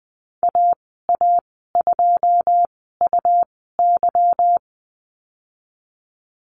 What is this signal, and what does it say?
Morse code, heard as a keyed tone, about 20 wpm, spaced as for 15 wpm, 700 Hz. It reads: AA2UY
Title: 179 Lesson 29g - Slash - Call Signs 20wpm